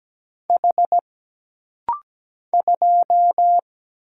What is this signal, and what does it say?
Morse code, heard as a keyed tone, about 17 wpm, 700 Hz. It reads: H 2
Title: Single Letter-Number - Rapid-Fire 17wpm